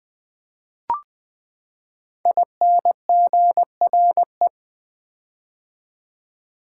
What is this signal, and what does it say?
Morse code, heard as a keyed tone, about 20 wpm, 700 Hz. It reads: INGRE
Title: Build A Word - 5 Letters At A Time - 10-Letter Words - 20wpm